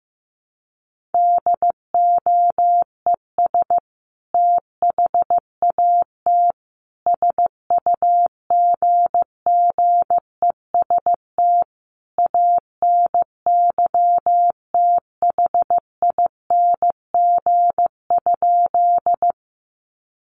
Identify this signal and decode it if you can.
Morse code, heard as a keyed tone, about 15 wpm, 700 Hz. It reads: DOES THAT SUGGEST ANYTHING?
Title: Sentences from Top 1300 Words 15wpm